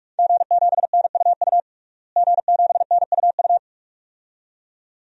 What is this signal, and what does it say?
Morse code, heard as a keyed tone, about 46 wpm, 700 Hz. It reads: G7NUU G7NUU
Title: DXCC-100 Call Signs - 5 Characters - Repeated 2x - 45wpm